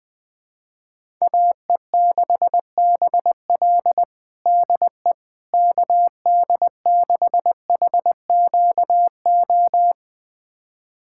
Morse code, heard as a keyed tone, about 20 wpm, 700 Hz. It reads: AE6BL DE KD6HQO
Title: USA Callsign DE Callsign 20wpm